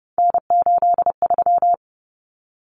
Morse code, heard as a keyed tone, about 30 wpm, 700 Hz. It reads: N83
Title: Common Cut Numbers - 3-Digit 30wpm